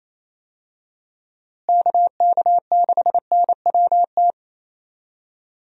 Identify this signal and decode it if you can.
Morse code, heard as a keyed tone, about 28 wpm, 700 Hz. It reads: KK6NWT
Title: USA Call Signs - 6 Characters - Repeated 2x - 28wpm